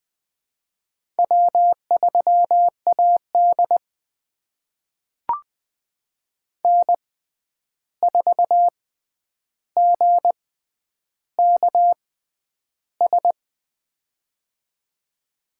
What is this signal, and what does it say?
Morse code, heard as a keyed tone, about 20 wpm, spaced as for 10 wpm, 700 Hz. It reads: W3AD N4GKS
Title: USA Call Signs - 6x Character Spacing - 20wpm